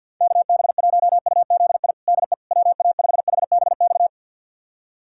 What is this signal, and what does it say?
Morse code, heard as a keyed tone, about 52 wpm, 700 Hz. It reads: KB1UZI DE WA5HBX
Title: USA Callsign DE Callsign 50wpm